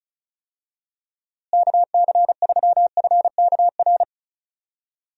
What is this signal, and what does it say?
Morse code, heard as a keyed tone, about 36 wpm, 700 Hz. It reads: KC3FKR
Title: USA Call Signs - 6 Characters - Repeated 2x - 35wpm